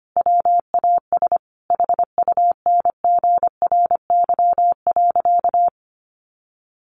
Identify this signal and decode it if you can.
Morse code, heard as a keyed tone, about 25 wpm, 700 Hz. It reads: WAS HUNGRY.
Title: Sentences from Top 1400 Words 25wpm